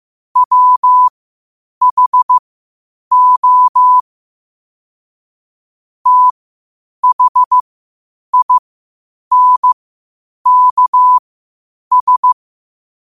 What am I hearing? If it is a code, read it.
Morse code, heard as a keyed tone, about 15 wpm, spaced as for 8 wpm, 1000 Hz. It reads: WHO THINKS
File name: qotdfiles / 8wpm